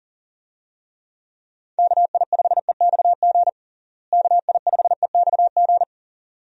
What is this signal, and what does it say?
Morse code, heard as a keyed tone, about 37 wpm, 700 Hz. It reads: KI5EXG KI5EXG
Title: USA Call Signs - 6 Characters - Repeated 2x - 40wpm